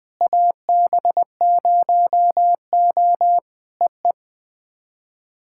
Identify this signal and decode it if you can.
Morse code, heard as a keyed tone, about 20 wpm, 700 Hz. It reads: AB0O EE
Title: POTA QSO Practice - 20wpm